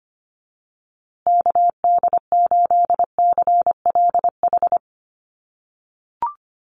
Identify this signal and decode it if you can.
Morse code, heard as a keyed tone, about 25 wpm, 700 Hz. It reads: KD8CLH
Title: USA Call Signs Repeated 2x 25wpm